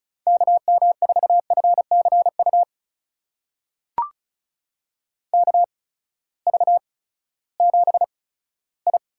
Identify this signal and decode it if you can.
Morse code, heard as a keyed tone, about 36 wpm, spaced as for 18 wpm, 700 Hz. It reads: KM4FCU KV7I
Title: USA Call Signs - 8x Character Spacing - 35wpm